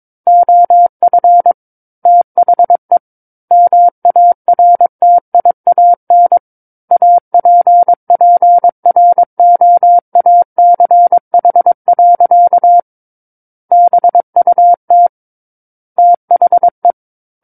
Morse code, heard as a keyed tone, about 22 wpm, 700 Hz. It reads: OF THE MARTIAN APPROACH. BUT THE
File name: Book One / 13-Chapter 13 - 22 WPM